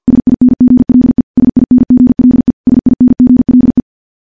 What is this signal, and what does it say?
Morse code, heard as a keyed tone, about 50 wpm, 250 Hz. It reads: SINGLE SINGLE SINGLE
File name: morse / morse-training-50WPM